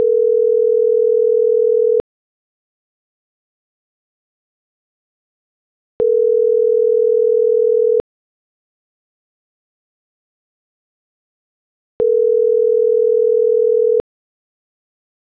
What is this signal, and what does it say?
Ringback tone (North America), 440 + 480 Hz.